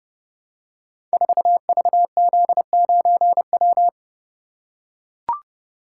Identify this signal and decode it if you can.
Morse code, heard as a keyed tone, about 30 wpm, 700 Hz. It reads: 4VZ9W
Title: Code Groups - 5 Letter:Numbers - 30wpm